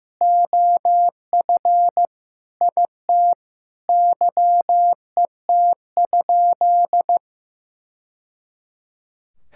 Morse code, heard as a keyed tone, about 15 wpm, 700 Hz. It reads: OF IT YET?
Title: Sentences from Top 1000 Words - Review Part 6 - 15wpm